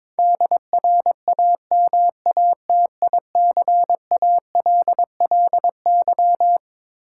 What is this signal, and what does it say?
Morse code, heard as a keyed tone, about 22 wpm, 700 Hz. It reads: DRAMATICALLY
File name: Twelve Letter Words - 22wpm